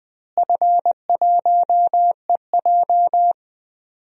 Morse code, heard as a keyed tone, about 20 wpm, 700 Hz. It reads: F1EJ
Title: DXCC-100 Call Signs - 4 Characters - Repeated 3x - 20wpm